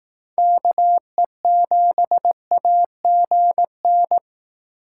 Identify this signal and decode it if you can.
Morse code, heard as a keyed tone, about 18 wpm, 700 Hz. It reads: KE7AGN
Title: Speed Racing - USA 2x3 Call Signs 15wpm